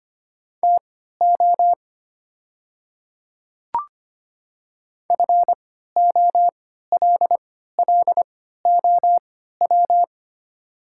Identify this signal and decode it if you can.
Morse code, heard as a keyed tone, about 25 wpm, spaced as for 14 wpm, 700 Hz. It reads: TO FOLLOW
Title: 3x Character Spacing - Top 300 Words - 25wpm